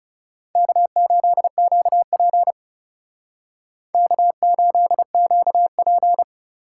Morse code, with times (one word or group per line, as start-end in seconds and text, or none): 0.6-2.5 K8QP
3.9-6.2 K8QP